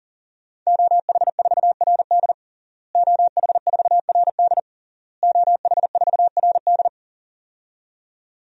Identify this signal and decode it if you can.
Morse code, heard as a keyed tone, about 40 wpm, 700 Hz. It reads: OH4RD OH4RD OH4RD
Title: DXCC-100 Call Signs - 5 Characters - Repeated 3x - 40wpm